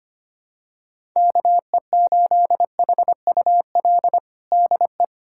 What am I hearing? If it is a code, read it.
Morse code, heard as a keyed tone, about 25 wpm, 700 Hz. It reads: KE8HUL DE
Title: USA Callsign DE Callsign 25wpm